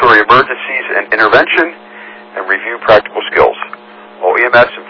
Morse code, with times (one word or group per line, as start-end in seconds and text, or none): none